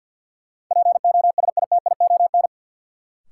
Morse code, heard as a keyed tone, about 48 wpm, 700 Hz. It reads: POSITION
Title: Sentences from Top 900 Words - New Word Drill - 50wpm